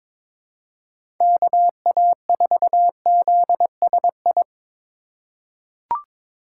Morse code, heard as a keyed tone, about 22 wpm, 700 Hz. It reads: KA4ZSI
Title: USA Call Signs - 6 Characters - 6x Character Spacing - 22wpm